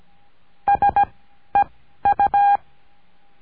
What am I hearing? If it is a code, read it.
Morse code, heard as a keyed tone, about 17 wpm, spaced as for 12 wpm, 810 Hz. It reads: SEU
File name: cursos / CURSO DE TELEGRAFIA / Telegrafia - Trilha 17